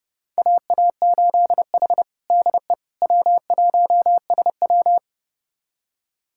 Morse code, heard as a keyed tone, about 32 wpm, 700 Hz. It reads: AA8H DE W1SW
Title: USA Callsign DE Callsign - 4 Characters - 30wpm